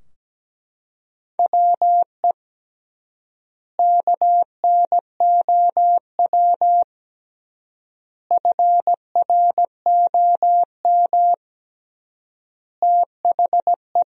Morse code, heard as a keyed tone, about 17 wpm, 700 Hz. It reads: WE KNOW FROM THE
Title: 3x Word-Spacing - Sentences from Top 500 Words - 17wpm